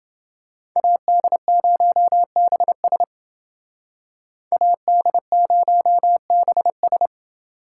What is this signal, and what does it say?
Morse code, heard as a keyed tone, about 28 wpm, 700 Hz. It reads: AD0BS AD0BS